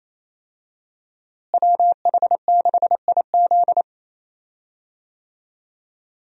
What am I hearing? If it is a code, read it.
Morse code, heard as a keyed tone, about 28 wpm, 700 Hz. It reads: WH6IZ
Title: USA Call Signs - 5 Characters - Repeated 2x - 28wpm